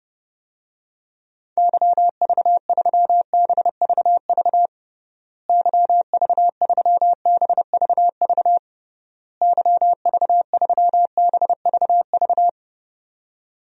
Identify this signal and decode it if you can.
Morse code, heard as a keyed tone, about 32 wpm, 700 Hz. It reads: YV3BVV YV3BVV YV3BVV
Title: DXCC-100 Call Signs - 6 Characters - Repeated 3x - 30wpm